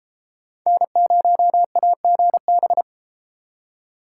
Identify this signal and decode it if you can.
Morse code, heard as a keyed tone, about 33 wpm, 700 Hz. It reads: N0AGB